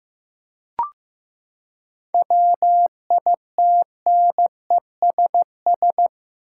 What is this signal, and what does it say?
Morse code, heard as a keyed tone, about 15 wpm, 700 Hz. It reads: WITNESS
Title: Seven Letter Words - 15wpm